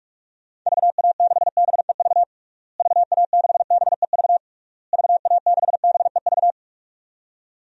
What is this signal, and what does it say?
Morse code, heard as a keyed tone, about 46 wpm, 700 Hz. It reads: VA6BEV VA6BEV VA6BEV
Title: DXCC-100 Call Signs - 6 Characters - Repeated 3x - 45wpm